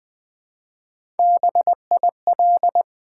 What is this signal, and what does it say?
Morse code, heard as a keyed tone, about 20 wpm, 700 Hz. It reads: BIL
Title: Build A Word - 1 Letter At A Time - 4-Letter Words - 20wpm